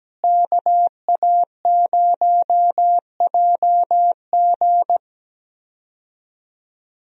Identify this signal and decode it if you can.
Morse code, heard as a keyed tone, about 17 wpm, 700 Hz. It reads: KA0JG
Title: USA Call Signs - 5 Characters - Repeated 3x - 17wpm